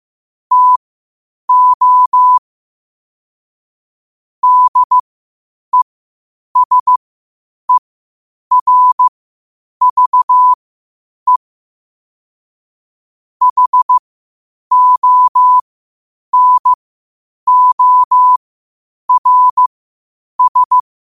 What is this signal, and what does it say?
Morse code, heard as a keyed tone, about 15 wpm, spaced as for 8 wpm, 1000 Hz. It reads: TO DESERVE HONORS
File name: qotdfiles / 8wpm